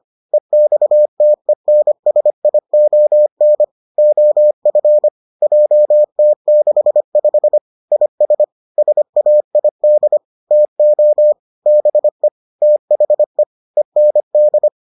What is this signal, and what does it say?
Morse code, heard as a keyed tone, about 25 wpm, 600 Hz. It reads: EXTENSION OF JT65 IS SAID TO BE THE END